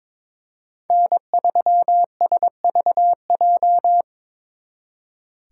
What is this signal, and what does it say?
Morse code, heard as a keyed tone, about 26 wpm, 700 Hz. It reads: N3SVJ
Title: Speed Racing - USA 1x3 Call Signs 22wpm